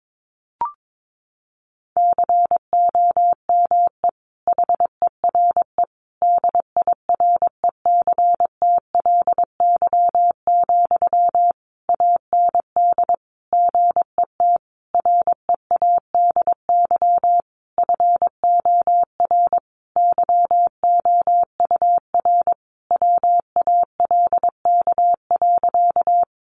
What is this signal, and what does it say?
Morse code, heard as a keyed tone, about 22 wpm, 700 Hz. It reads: COME HERE DIRECTLY, AND GET READY FOR YOUR WALK.
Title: Sentences from Top 800 Words 22wpm